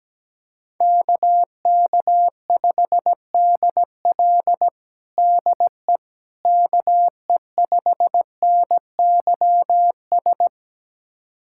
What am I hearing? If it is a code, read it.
Morse code, heard as a keyed tone, about 17 wpm, 700 Hz. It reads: KK5DL DE KE5NYS
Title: USA Callsign DE Callsign 17wpm